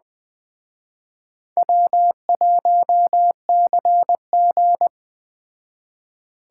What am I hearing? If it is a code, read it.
Morse code, heard as a keyed tone, about 21 wpm, 700 Hz. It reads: W1CG
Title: Speed Racing - USA Call Signs 17wpm